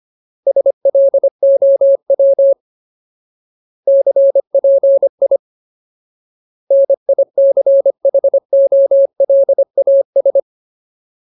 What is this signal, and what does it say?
Morse code, heard as a keyed tone, about 25 wpm, 550 Hz. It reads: SLOW CPI NICHOLAS
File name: Ultimate Word List / 25WPM Ultimate